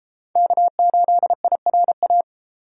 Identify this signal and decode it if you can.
Morse code, heard as a keyed tone, about 30 wpm, 700 Hz. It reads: K8IRA